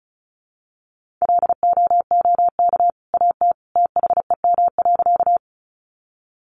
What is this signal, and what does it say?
Morse code, heard as a keyed tone, about 34 wpm, 700 Hz. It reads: LOOK AT THEM.
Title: Sentences from Top 100 Words 35wpm